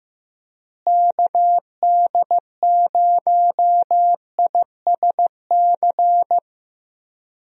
Morse code, heard as a keyed tone, about 15 wpm, 700 Hz. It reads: KD0ISC
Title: USA Call Signs - 6 Characters - Repeated 3x - 15wpm